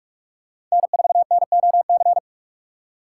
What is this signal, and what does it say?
Morse code, heard as a keyed tone, about 47 wpm, 690 Hz. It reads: N4NOC
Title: USA Call Signs - 5 Characters - 8x Character Spacing - 45wpm